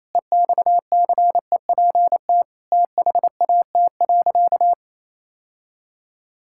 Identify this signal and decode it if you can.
Morse code, heard as a keyed tone, about 28 wpm, 700 Hz. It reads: EXCEPT THAT.
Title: Sentences from Top 1600 Words 28wpm